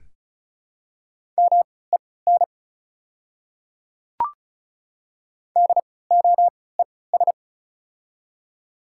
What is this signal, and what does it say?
Morse code, heard as a keyed tone, about 36 wpm, spaced as for 20 wpm, 700 Hz. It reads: MEN DOES
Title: 3x Character Spacing - Top 200 Words - 35wpm